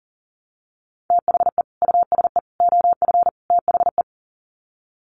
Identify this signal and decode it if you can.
Morse code, heard as a keyed tone, about 40 wpm, 700 Hz. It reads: THE USE OF THE
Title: Sets of 4 Words 40wpm